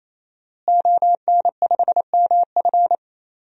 Morse code, heard as a keyed tone, about 28 wpm, 700 Hz. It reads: ON5MF
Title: DXCC-100 Call Signs - 5 Characters - Repeated 3x - 28wpm